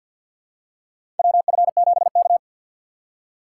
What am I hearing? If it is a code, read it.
Morse code, heard as a keyed tone, about 49 wpm, 700 Hz. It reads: WF7K
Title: USA Call Signs - 4 Characters - 6x Character Spacing - 50wpm